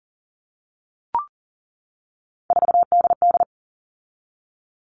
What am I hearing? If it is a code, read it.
Morse code, heard as a keyed tone, about 40 wpm, 700 Hz. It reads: 4DD
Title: All Cut Numbers - 3-Digit 40wpm